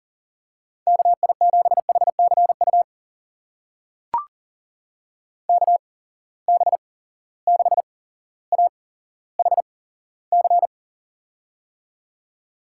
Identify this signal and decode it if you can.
Morse code, heard as a keyed tone, about 40 wpm, spaced as for 17 wpm, 700 Hz. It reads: KI7HCU KB6AHC
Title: USA Call Signs - 6 Characters - 8x Character Spacing - 40wpm